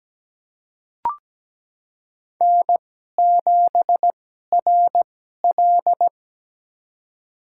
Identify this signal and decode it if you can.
Morse code, heard as a keyed tone, about 17 wpm, spaced as for 12 wpm, 700 Hz. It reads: N7RL
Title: USA Call Signs - 2x Character Spacing - 17wpm